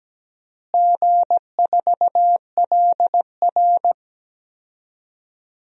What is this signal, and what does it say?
Morse code, heard as a keyed tone, about 17 wpm, 700 Hz. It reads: G4LR
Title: DXCC-100 Call Signs - 4 Characters - Repeated 2x - 17wpm